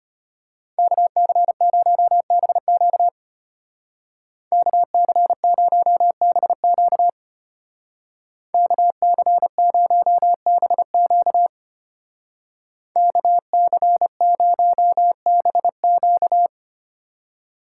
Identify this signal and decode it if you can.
Morse code, heard as a keyed tone, about 30 wpm, 700 Hz. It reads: KC0BQ KC0BQ KC0BQ KC0BQ
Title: Speed Racing - USA 2x2 Call Signs 25wpm